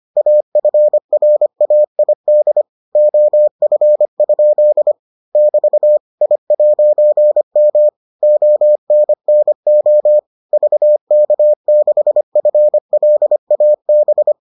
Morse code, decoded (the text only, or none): AFRAID OF? <BT> I'M ONNO VK6FLAB